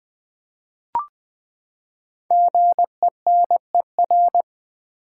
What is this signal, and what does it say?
Morse code, heard as a keyed tone, about 20 wpm, 700 Hz. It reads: GENER
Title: Build A Word - 5 Letters At A Time - 10-Letter Words - 20wpm